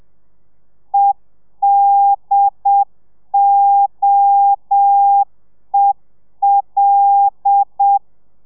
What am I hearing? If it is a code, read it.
Morse code, heard as a keyed tone, about 7 wpm, 800 Hz. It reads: EDOEL
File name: telegrafia / i2tfj / Lez5 30